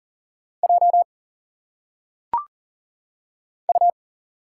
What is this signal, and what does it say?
Morse code, heard as a keyed tone, about 40 wpm, 700 Hz. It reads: J U
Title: Single Letters - Phonetic Pronunciation 40wpm